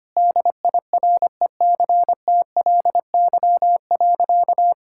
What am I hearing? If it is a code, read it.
Morse code, heard as a keyed tone, about 25 wpm, 700 Hz. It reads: DIRECTLY.
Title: Sentences from Top 1000 Words - Review Part 5 - 25wpm